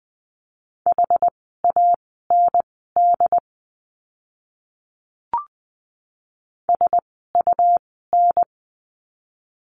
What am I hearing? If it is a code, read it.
Morse code, heard as a keyed tone, about 20 wpm, spaced as for 14 wpm, 700 Hz. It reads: HAND SUN
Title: 101 Lesson 17f - Letter u - Words 20wpm